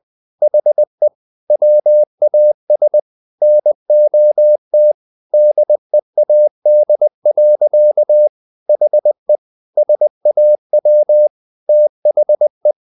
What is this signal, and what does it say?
Morse code, heard as a keyed tone, about 20 wpm, 600 Hz. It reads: HE WAS NOT DEAD. HE SAW THE
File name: the old man and the sea part 2 20wpm